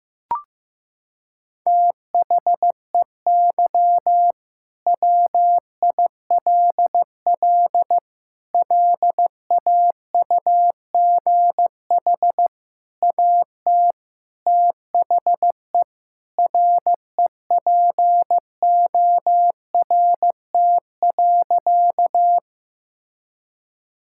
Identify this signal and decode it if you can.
Morse code, heard as a keyed tone, about 15 wpm, 700 Hz. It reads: THEY WILL LAUGH AT THE REPORT.
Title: Sentences from Top 1000 Words - Review Part 6 - 15wpm